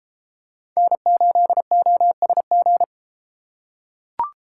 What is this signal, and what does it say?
Morse code, heard as a keyed tone, about 31 wpm, 700 Hz. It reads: N8OSG